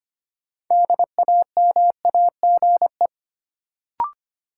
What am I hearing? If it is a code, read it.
Morse code, heard as a keyed tone, about 25 wpm, 700 Hz. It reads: DAMAGE
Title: Sentences from Top 1100 Words - New Word Drill Alt - 25wpm